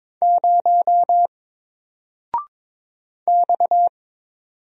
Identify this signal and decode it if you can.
Morse code, heard as a keyed tone, about 22 wpm, 700 Hz. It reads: 0 X
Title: Single Letter-Number Alt 22wpm